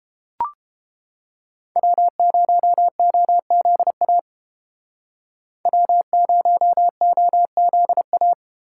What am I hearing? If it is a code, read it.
Morse code, heard as a keyed tone, about 32 wpm, 700 Hz. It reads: W0OZA W0OZA